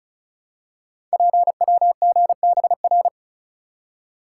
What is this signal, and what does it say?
Morse code, heard as a keyed tone, about 36 wpm, 700 Hz. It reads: PWGBR
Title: Code Groups - 5 Letters - 35wpm